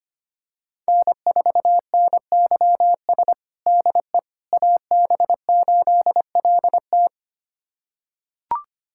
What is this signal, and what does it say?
Morse code, heard as a keyed tone, about 25 wpm, 700 Hz. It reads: N4NYS DE AB8LT
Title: USA Callsign DE Callsign - 5 Characters - 25wpm